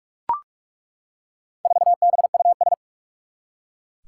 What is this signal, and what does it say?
Morse code, heard as a keyed tone, about 41 wpm, 700 Hz. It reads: 4BUS
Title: Four Challenging Letter-Number Combinations 45wpm